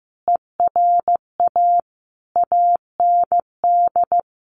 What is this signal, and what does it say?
Morse code, heard as a keyed tone, about 15 wpm, 700 Hz. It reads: ERA AND
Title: A Sound Of Thunder 15wpm